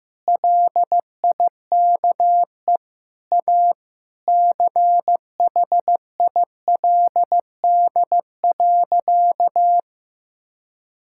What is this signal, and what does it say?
Morse code, heard as a keyed tone, about 15 wpm, 700 Hz. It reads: LIKE A CHILD.
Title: Sentences from Top 1000 Words - Review Part 6 - 15wpm